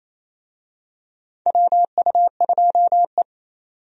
Morse code, heard as a keyed tone, about 28 wpm, 700 Hz. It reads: WU2E
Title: USA Call Signs - 4 Characters - 28wpm